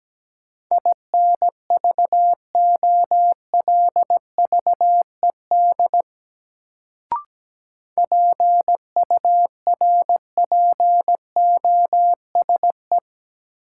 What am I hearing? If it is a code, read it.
Morse code, heard as a keyed tone, about 17 wpm, 700 Hz. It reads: INVOLVED PURPOSE
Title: Sentences from Top 800 Words - New Word Drill - 17wpm